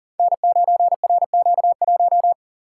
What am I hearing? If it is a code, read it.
Morse code, heard as a keyed tone, about 40 wpm, 700 Hz. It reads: N9RQ1
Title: Code Groups - 5 Letter:Numbers - 40wpm